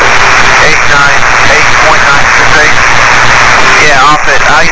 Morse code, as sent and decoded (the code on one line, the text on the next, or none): none